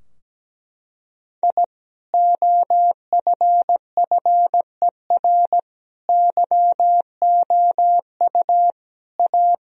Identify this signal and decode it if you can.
Morse code, heard as a keyed tone, about 17 wpm, 700 Hz. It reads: I OFFER YOU A
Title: Sentences from Top 1000 Words - Review Part 4 - 17wpm